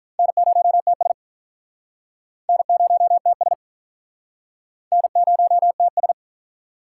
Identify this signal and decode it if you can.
Morse code, heard as a keyed tone, about 47 wpm, 700 Hz. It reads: N0TS N0TS N0TS